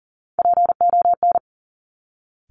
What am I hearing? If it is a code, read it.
Morse code, heard as a keyed tone, about 38 wpm, 700 Hz. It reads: PON
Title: Top Three Letter Combinations in Words 40wpm